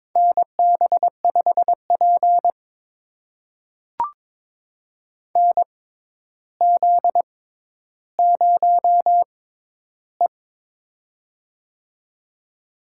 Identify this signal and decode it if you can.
Morse code, heard as a keyed tone, about 22 wpm, spaced as for 11 wpm, 700 Hz. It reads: NB5P NZ0E
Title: USA Call Signs - 4 Characters - 6x Character Spacing - 22wpm